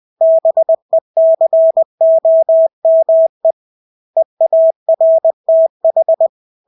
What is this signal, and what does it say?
Morse code, heard as a keyed tone, about 20 wpm, 650 Hz. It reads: BECOME EARTH
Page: Engelske ord 20 wpm